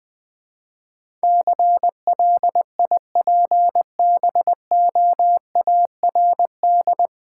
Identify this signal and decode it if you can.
Morse code, heard as a keyed tone, about 21 wpm, 700 Hz. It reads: CLIPBOARD
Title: Easy Words with Hints - 20wpm